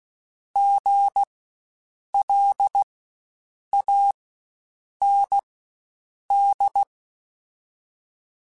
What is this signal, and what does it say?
Morse code, heard as a keyed tone, about 16 wpm, spaced as for 7 wpm, 800 Hz. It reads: GLAND
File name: Spaced Alphas / Spaced Alphas mp3s / Gs sp @15